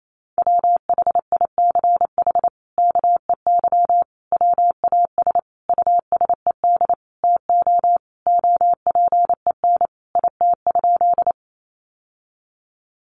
Morse code, heard as a keyed tone, about 28 wpm, 700 Hz. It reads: WHICH KEY WAS USED TO OPEN IT?
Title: Sentences from Top 600 Words 28wpm